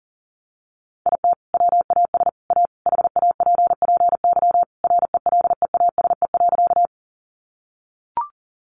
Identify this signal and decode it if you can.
Morse code, heard as a keyed tone, about 40 wpm, 700 Hz. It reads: IT WAS A HAPPY RELEASE.
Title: Sentences from Top 1100 Words 40wpm